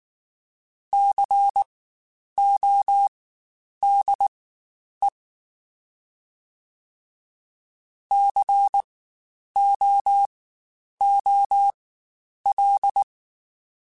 Morse code, heard as a keyed tone, about 19 wpm, spaced as for 9 wpm, 800 Hz. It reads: CODE COOL
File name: Spaced Alphas / Spaced Alphas mp3s / Cs sp @18